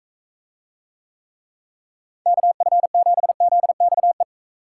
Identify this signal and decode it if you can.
Morse code, heard as a keyed tone, about 44 wpm, 700 Hz. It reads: KF7ZXE